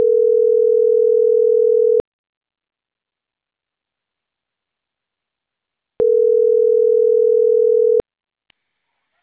Ringback tone (North America), 440 + 480 Hz.